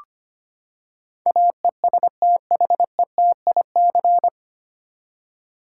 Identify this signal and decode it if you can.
Morse code, heard as a keyed tone, about 25 wpm, 700 Hz. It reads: AESTHETIC
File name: Nine Letter Words - 25wpm